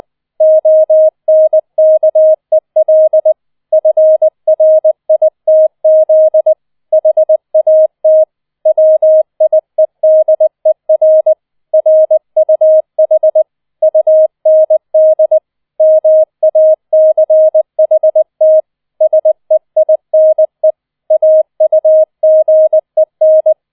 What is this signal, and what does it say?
Morse code, heard as a keyed tone, about 20 wpm, 630 Hz. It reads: ONKEL FRITZ HAT WIEDER RUH UND MACHT SEINE AUGEN
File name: 20wpm